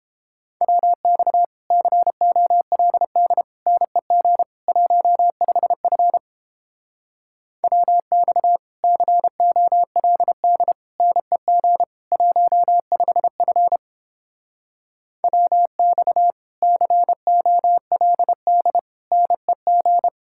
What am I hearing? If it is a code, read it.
Morse code, heard as a keyed tone, about 29 wpm, 700 Hz. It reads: WX COLD NEG 15F WX COLD NEG 15F WX COLD NEG